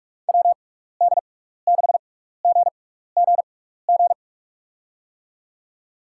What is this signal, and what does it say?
Morse code, heard as a keyed tone, about 46 wpm, spaced as for 16 wpm, 700 Hz. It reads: WD6GGG
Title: USA Call Signs - 6 Characters - 6x Character Spacing - 45wpm